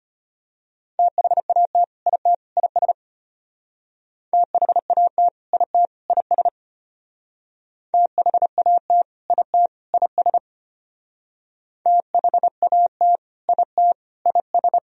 Morse code, heard as a keyed tone, about 33 wpm, 700 Hz. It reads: THAT IT IS THAT IT IS THAT IT IS THAT IT IS